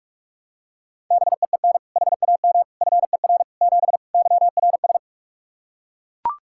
Morse code, heard as a keyed tone, about 41 wpm, 700 Hz. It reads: BEEN HAM FER 7 YRS
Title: K1USN SST QSO Practice 45wpm